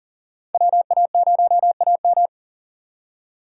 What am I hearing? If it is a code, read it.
Morse code, heard as a keyed tone, about 40 wpm, 700 Hz. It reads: WA0AM
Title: USA Call Signs - 5 Characters - 6x Character Spacing - 40wpm